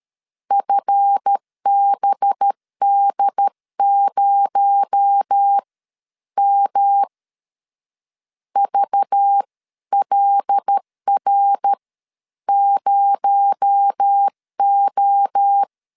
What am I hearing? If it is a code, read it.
Morse code, heard as a keyed tone, about 13 wpm, spaced as for 10 wpm, 800 Hz. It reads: FBD0M VLR0O
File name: LEZIONI / 7 GF9Z / 7 GF 12wpm 2122